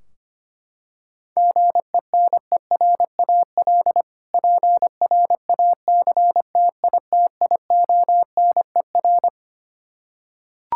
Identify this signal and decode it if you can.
Morse code, heard as a keyed tone, about 25 wpm, 700 Hz. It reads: GENERAL PRACTITIONER
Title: Practice QSO - Exchange 3 - More Info - 25wpm